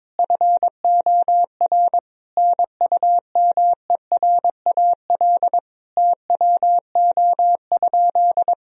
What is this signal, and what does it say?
Morse code, heard as a keyed tone, about 22 wpm, 700 Hz. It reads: FOR NUMERAL TWO?
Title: Sentences from Top 1000 Words - Review Part 2 - 22wpm